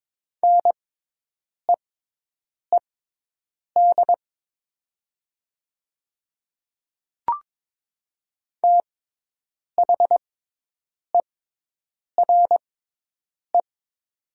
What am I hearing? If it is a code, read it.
Morse code, heard as a keyed tone, about 22 wpm, spaced as for 8 wpm, 700 Hz. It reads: NEED THERE
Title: 6x Character Spacing - Top 200 Words - 22wpm